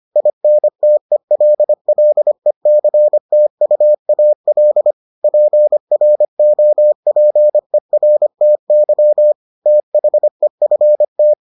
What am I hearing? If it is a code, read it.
Morse code, heard as a keyed tone, about 25 wpm, 600 Hz. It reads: INTELLECTUAL PROPERTY THEFT